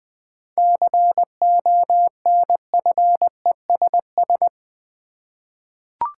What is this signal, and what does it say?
Morse code, heard as a keyed tone, about 20 wpm, 700 Hz. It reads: CONFESS
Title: 7 Letter Words - First 3 Letters Given - 20wpm